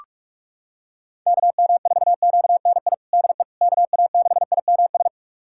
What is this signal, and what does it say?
Morse code, heard as a keyed tone, about 47 wpm, 700 Hz. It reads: KM4QNI DE KA6IMS
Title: USA Callsign DE Callsign - 6 Characters - 45wpm